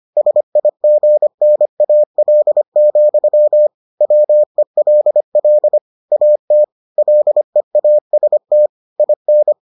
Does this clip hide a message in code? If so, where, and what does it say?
Yes: Morse code from 0.2 to 9.5 s, heard as a keyed tone, about 25 wpm, 600 Hz. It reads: SIGNAL, WELL AT LEAST IN